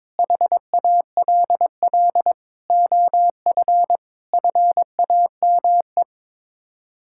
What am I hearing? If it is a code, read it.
Morse code, heard as a keyed tone, about 22 wpm, 700 Hz. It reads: HALL OF FAME
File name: practice / 22wpm / Sets of 7 Words - 22wpm